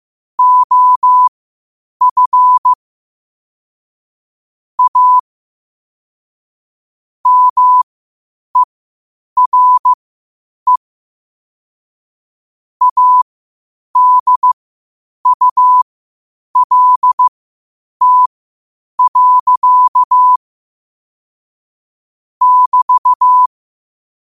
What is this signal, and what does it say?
Morse code, heard as a keyed tone, about 15 wpm, spaced as for 8 wpm, 1000 Hz. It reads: OF A MERE ADULT. <BT>
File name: qotdfiles / 8wpm